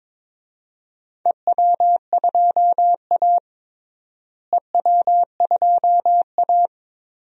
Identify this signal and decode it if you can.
Morse code, heard as a keyed tone, about 22 wpm, 700 Hz. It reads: EW2A EW2A
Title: DXCC-100 Call Signs - 4 Characters - Repeated 2x - 22wpm